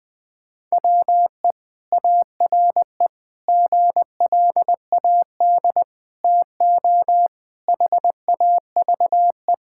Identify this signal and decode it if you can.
Morse code, heard as a keyed tone, about 20 wpm, 700 Hz. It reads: WE ARE GLAD TO HAVE
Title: Sentences with 5-Word Phrases - 20wpm